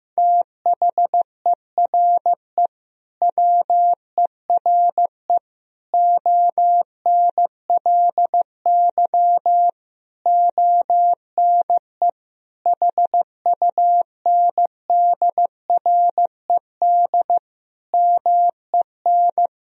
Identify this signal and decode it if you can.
Morse code, heard as a keyed tone, about 16 wpm, 700 Hz. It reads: THERE WERE ONLY ONE HUNDRED MEN
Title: Sentences from Top 1000 Words - Review Part 2 - 15wpm